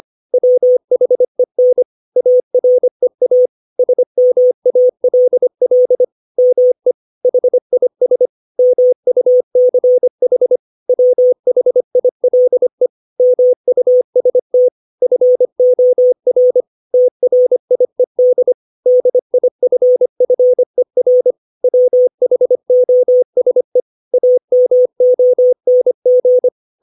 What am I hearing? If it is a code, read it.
Morse code, heard as a keyed tone, about 25 wpm, 500 Hz. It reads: WHEN AREA SMALL ME HIS MUCH WHILE MUST FOR TRIED DIFFER WHOSE AMONG
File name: 632 words / 632 Words 25WPM